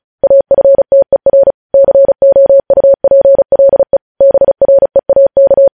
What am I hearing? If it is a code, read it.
Morse code, heard as a keyed tone, about 36 wpm, 550 Hz. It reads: AFTER COUPLE BREAK
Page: news headlines in morse code at 35 wpm